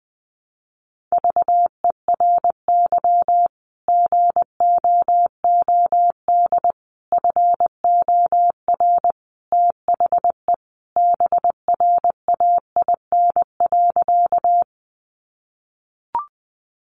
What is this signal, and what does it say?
Morse code, heard as a keyed tone, about 20 wpm, 700 Hz. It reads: VERY GOOD FOR THE BRAIN.
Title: Sentences from Top 1100 Words 20wpm